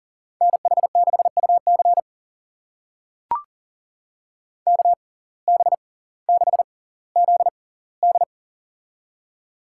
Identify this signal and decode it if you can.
Morse code, heard as a keyed tone, about 39 wpm, spaced as for 20 wpm, 700 Hz. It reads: NH6UC KB6ZD
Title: USA Call Signs - 5 Characters - 6x Character Spacing - 40wpm